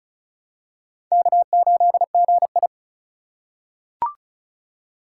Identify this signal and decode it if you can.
Morse code, heard as a keyed tone, about 36 wpm, 700 Hz. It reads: K8GI
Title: USA Call Signs - 4 Characters - 3x Character Spacing - 35wpm